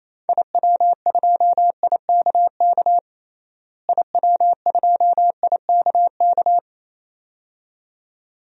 Morse code, heard as a keyed tone, about 27 wpm, 700 Hz. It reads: IW2IKK IW2IKK
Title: DXCC-100 Call Signs - 6 Characters - Repeated 2x - 28wpm